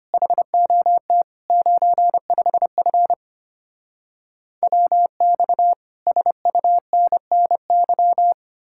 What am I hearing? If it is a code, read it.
Morse code, heard as a keyed tone, about 33 wpm, 700 Hz. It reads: HOT 95F WX SUNNY